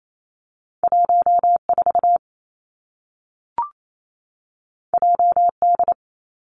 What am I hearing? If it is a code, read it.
Morse code, heard as a keyed tone, about 28 wpm, 700 Hz. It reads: 1 4 J D
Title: Two Challenging Letter-Number Combinations 28wpm